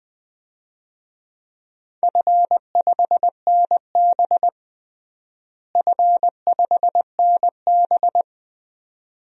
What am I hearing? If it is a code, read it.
Morse code, heard as a keyed tone, about 20 wpm, 700 Hz. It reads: F5NB F5NB
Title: DXCC-100 Call Signs - 4 Characters - Repeated 2x - 20wpm